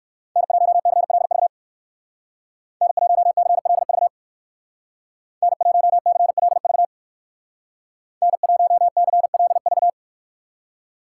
Morse code, heard as a keyed tone, about 59 wpm, 700 Hz. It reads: N1CLV N1CLV N1CLV N1CLV